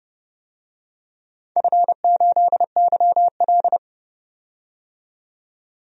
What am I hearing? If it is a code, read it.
Morse code, heard as a keyed tone, about 30 wpm, 700 Hz. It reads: F8YL
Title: DXCC-100 Call Signs - 4 Characters - 30wpm